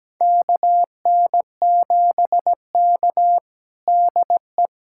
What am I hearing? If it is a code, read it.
Morse code, heard as a keyed tone, about 17 wpm, 700 Hz. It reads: KN7K DE
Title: USA Callsign DE Callsign - 4 Characters - 17wpm